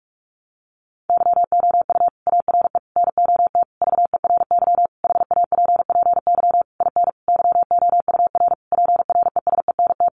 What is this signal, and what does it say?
Morse code, heard as a keyed tone, about 44 wpm, 700 Hz. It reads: YOU ARE NOT VERY HAPPY IN YOUR PRESENT
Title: Sentences from Top 1200 Words 45wpm